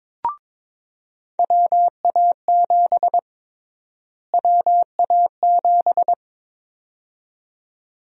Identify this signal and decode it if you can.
Morse code, heard as a keyed tone, about 22 wpm, 700 Hz. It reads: WA7 WA7
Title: USA Call Sign Prefixes - Repeated 2x - 22wpm